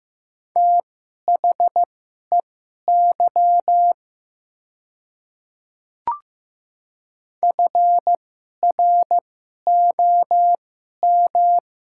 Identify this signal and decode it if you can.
Morse code, heard as a keyed tone, about 15 wpm, spaced as for 11 wpm, 690 Hz. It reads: THEY FROM
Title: 2x Character Spacing - Top 100 Words - 15wpm